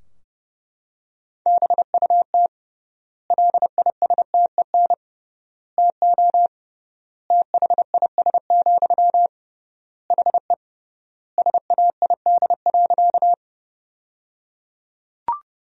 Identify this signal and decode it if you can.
Morse code, heard as a keyed tone, about 30 wpm, 700 Hz. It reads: BUT LISTEN TO THIS, HE SAID.
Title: 3x Word-Spacing - Sentences from Top 400 Words - 30wpm